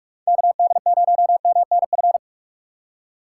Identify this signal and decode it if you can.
Morse code, heard as a keyed tone, about 43 wpm, 700 Hz. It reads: KD0MNF